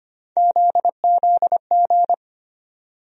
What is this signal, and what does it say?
Morse code, heard as a keyed tone, about 25 wpm, 700 Hz. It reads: ZZG
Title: USA Call Sign Postfixes - 3 Letters 25wpm